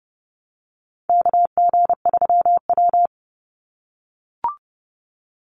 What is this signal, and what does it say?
Morse code, heard as a keyed tone, about 30 wpm, 700 Hz. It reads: KG3W
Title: NA SOTA Chaser Callsigns 30wpm